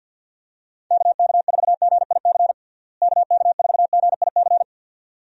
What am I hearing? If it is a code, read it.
Morse code, heard as a keyed tone, about 52 wpm, 700 Hz. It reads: KK4GIC KK4GIC
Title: USA Call Signs - 6 Characters - Repeated 2x - 50wpm